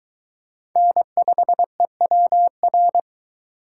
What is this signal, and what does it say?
Morse code, heard as a keyed tone, about 22 wpm, 700 Hz. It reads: N5EWR